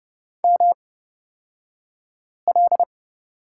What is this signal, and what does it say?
Morse code, heard as a keyed tone, about 30 wpm, spaced as for 5 wpm, 700 Hz. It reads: ML
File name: Lesson 38d - Letter q - Two Characters 30wpm